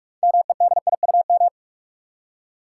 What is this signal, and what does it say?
Morse code, heard as a keyed tone, about 46 wpm, 700 Hz. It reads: MEDIUM
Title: Six Letter Words - 45wpm